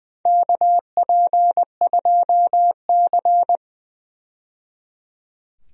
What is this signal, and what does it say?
Morse code, heard as a keyed tone, about 20 wpm, 700 Hz. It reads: KP2C
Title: DXCC-100 Call Signs - 4 Characters - 20wpm